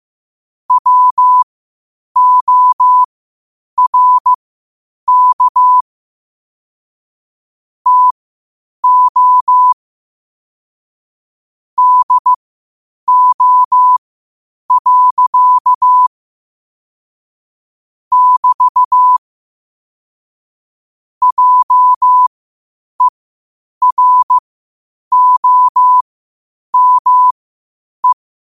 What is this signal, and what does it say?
Morse code, heard as a keyed tone, about 15 wpm, spaced as for 8 wpm, 1000 Hz. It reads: WORK TO DO. <BT> JEROME
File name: qotdfiles / 8wpm